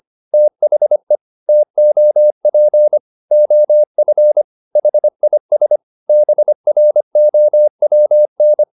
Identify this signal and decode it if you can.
Morse code, heard as a keyed tone, about 25 wpm, 600 Hz. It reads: THE TOP OF HIS BROWN